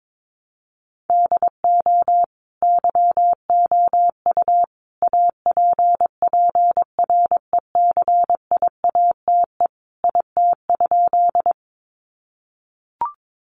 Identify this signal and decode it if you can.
Morse code, heard as a keyed tone, about 22 wpm, 700 Hz. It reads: DO YOU APPRECIATE IT?
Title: Sentences from Top 1300 Words 22wpm